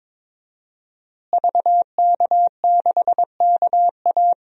Morse code, heard as a keyed tone, about 22 wpm, 700 Hz. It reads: VK6KA
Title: DXCC-100 Call Signs - 5 Characters - Repeated 2x - 22wpm